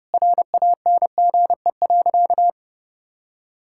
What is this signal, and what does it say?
Morse code, heard as a keyed tone, about 30 wpm, 700 Hz. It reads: RANGE.
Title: Sentences from Top 1000 Words - Review Part 6 - 30wpm